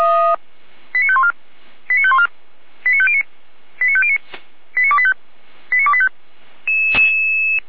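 Five-tone selective call (ZVEI), heard as quick runs of five tones, tones of about 70 ms. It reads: 89514 89514 89509 89509 89285 89285
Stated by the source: ZVEI five-tone selective call received by SDR.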